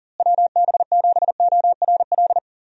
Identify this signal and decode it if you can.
Morse code, heard as a keyed tone, about 40 wpm, 700 Hz. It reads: WB7ORL
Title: USA Call Signs - 6 Characters - Repeated 3x - 40wpm